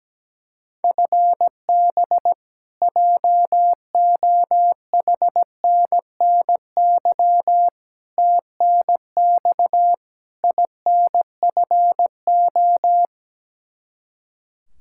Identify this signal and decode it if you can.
Morse code, heard as a keyed tone, about 17 wpm, 700 Hz. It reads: FB JOHNNY TNX INFO
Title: Practice QSO - Exchange 3 - More Info - 17wpm